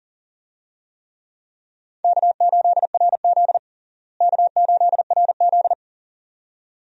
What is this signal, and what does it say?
Morse code, heard as a keyed tone, about 40 wpm, 700 Hz. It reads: K8RZ K8RZ
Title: USA Call Signs - 4 Characters - Repeated 2x - 40wpm